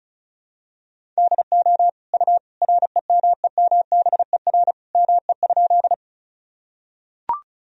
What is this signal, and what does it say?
Morse code, heard as a keyed tone, about 36 wpm, 700 Hz. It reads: DO U REMEMBER ME?
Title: Sentences from Top 400 Words and Abbreviations 35wpm